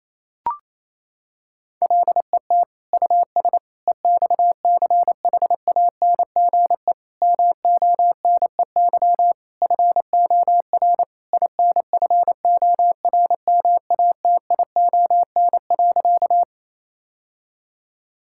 Morse code, heard as a keyed tone, about 28 wpm, 700 Hz. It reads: LET US EXCHANGE MONEY FOR INFORMATION.
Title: Sentences from Top 1000 Words - Review Part 6 - 28wpm